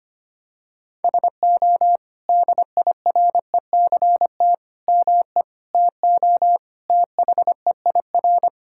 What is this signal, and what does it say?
Morse code, heard as a keyed tone, about 25 wpm, 700 Hz. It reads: SO DIRECT ME TO THEIR
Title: Sentences from Top 1600 Words 25wpm